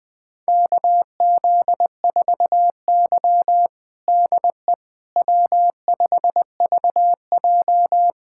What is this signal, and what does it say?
Morse code, heard as a keyed tone, about 20 wpm, 700 Hz. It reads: KZ4Y DE W5VJ
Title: USA Callsign DE Callsign - 4 Characters - 20wpm